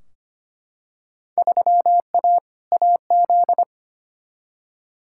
Morse code, heard as a keyed tone, about 25 wpm, 700 Hz. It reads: 3A AZ
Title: Field Day Exchanges - 25wpm